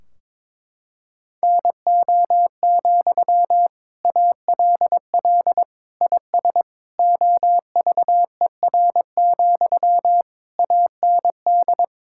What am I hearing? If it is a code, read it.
Morse code, heard as a keyed tone, about 22 wpm, 700 Hz. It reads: NO, ALL IS OVER, AND
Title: Sentences from Top 1600 Words 22wpm